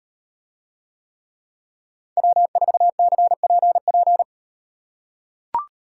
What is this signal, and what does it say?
Morse code, heard as a keyed tone, about 38 wpm, 700 Hz. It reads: W4CPP